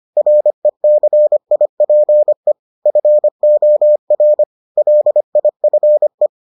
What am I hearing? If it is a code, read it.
Morse code, heard as a keyed tone, about 25 wpm, 600 Hz. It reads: RECIPE FOR LIFE